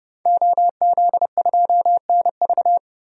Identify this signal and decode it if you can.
Morse code, heard as a keyed tone, about 30 wpm, 700 Hz. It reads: OZ2NV
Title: DXCC-100 Call Signs - 5 Characters - 30wpm